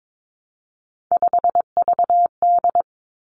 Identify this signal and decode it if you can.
Morse code, heard as a keyed tone, about 22 wpm, 700 Hz. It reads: 5VD
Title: Three Challenging Letter-Number Combinations 22wpm